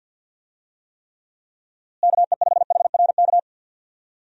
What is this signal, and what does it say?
Morse code, heard as a keyed tone, about 51 wpm, 700 Hz. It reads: KE5HRK
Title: USA Call Signs - 6 Characters - Repeated 2x - 50wpm